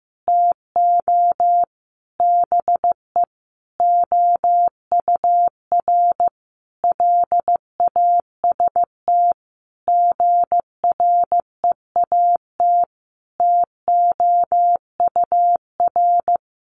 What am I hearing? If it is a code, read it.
Morse code, heard as a keyed tone, about 15 wpm, 700 Hz. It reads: TO BE OUR LAST GREAT TOUR
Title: Sentences from Top 1400 Words 15wpm